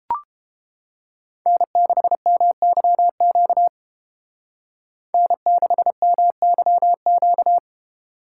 Morse code, heard as a keyed tone, about 32 wpm, 700 Hz. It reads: N6MYQ N6MYQ